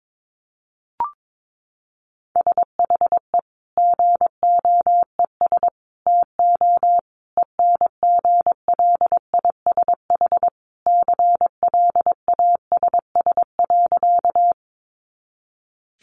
Morse code, heard as a keyed tone, about 22 wpm, 700 Hz. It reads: SHE GOES TO ENGLISH CLASS.
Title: Additional Easy Phrases 22wpm